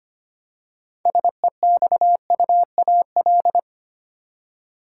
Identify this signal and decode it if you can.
Morse code, heard as a keyed tone, about 25 wpm, 700 Hz. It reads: SEXUAL
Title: Six Letter Words - 25wpm